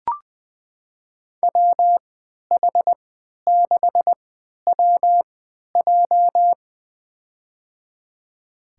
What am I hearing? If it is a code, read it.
Morse code, heard as a keyed tone, about 20 wpm, spaced as for 11 wpm, 700 Hz. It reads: WH6WJ
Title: USA Call Signs - 5 Characters - 3x Character Spacing - 20wpm